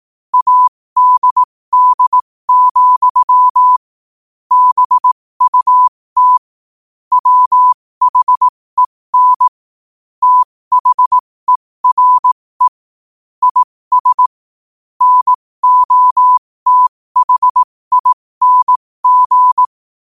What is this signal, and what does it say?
Morse code, heard as a keyed tone, about 18 wpm, spaced as for 15 wpm, 1000 Hz. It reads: ADD, BUT WHEN THERE IS NOTHING
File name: qotdfiles / 15wpm